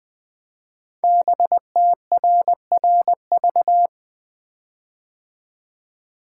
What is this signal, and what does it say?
Morse code, heard as a keyed tone, about 20 wpm, 700 Hz. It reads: BTRRV
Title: Code Groups - 5 Letters - 20wpm